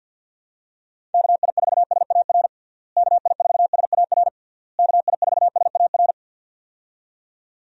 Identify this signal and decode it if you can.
Morse code, heard as a keyed tone, about 53 wpm, 700 Hz. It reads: KI4SAR KI4SAR KI4SAR
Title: USA Call Signs - 6 Characters - Repeated 3x - 50wpm